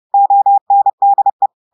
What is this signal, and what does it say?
Morse code, heard as a keyed tone, about 30 wpm, 800 Hz. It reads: ONDE
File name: audios / portugais / portugais texte en clair 1 vitesse 30 mots minutes